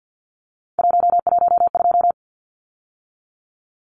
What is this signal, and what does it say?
Morse code, heard as a keyed tone, about 50 wpm, 700 Hz. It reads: 112
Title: Numbers - 3-Digit 50wpm